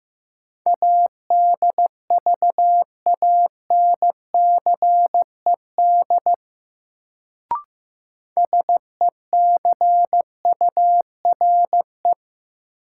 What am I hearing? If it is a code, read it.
Morse code, heard as a keyed tone, about 15 wpm, 700 Hz. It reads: ADVANCED SECURE
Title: Sentences from Top 1200 Words - New Word Drill - 15wpm